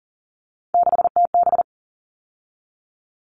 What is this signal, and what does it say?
Morse code, heard as a keyed tone, about 40 wpm, 700 Hz. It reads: 6TB
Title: All Cut Numbers - 3-Digit 40wpm